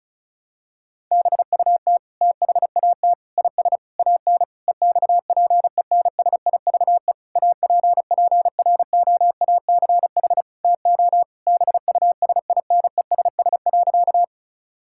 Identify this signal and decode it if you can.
Morse code, heard as a keyed tone, about 36 wpm, 700 Hz. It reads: BUT THAT IS AN EXPENSIVE APPROACH TO BUSINESS.
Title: Sentences from Top 1000 Words - Review Part 6 - 35wpm